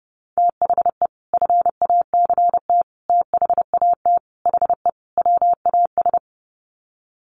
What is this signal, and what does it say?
Morse code, heard as a keyed tone, about 30 wpm, 700 Hz. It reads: THE FACT THAT HE WAS
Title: Sets of 5 Words 30wpm